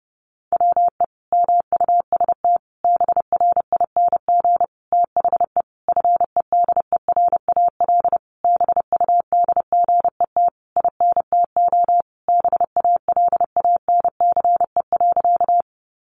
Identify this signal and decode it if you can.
Morse code, heard as a keyed tone, about 30 wpm, 700 Hz. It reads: WE MUST BRING THE FEDERAL BUDGET INTO BALANCE.
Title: Sentences from Top 1200 Words 30wpm